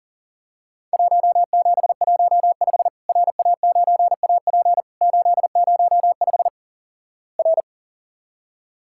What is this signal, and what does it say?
Morse code, heard as a keyed tone, about 43 wpm, 700 Hz. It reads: 1715 RA9AP 805 R
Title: Worked All Europe DX Contest - QTC Exchange - 40wpm